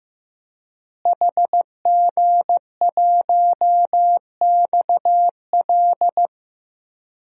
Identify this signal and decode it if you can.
Morse code, heard as a keyed tone, about 15 wpm, 700 Hz. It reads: HG1XL
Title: DXCC-100 Call Signs - 5 Characters - Repeated 2x - 15wpm